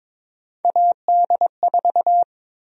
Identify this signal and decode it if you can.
Morse code, heard as a keyed tone, about 23 wpm, 700 Hz. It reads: AD4
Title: USA Call Sign Prefixes - 22wpm